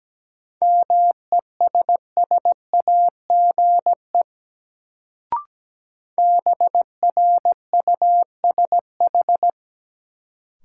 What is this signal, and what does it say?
Morse code, heard as a keyed tone, about 17 wpm, 700 Hz. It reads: MESSAGE BRUSH
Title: Sentences from Top 1100 Words - New Word Drill - 17wpm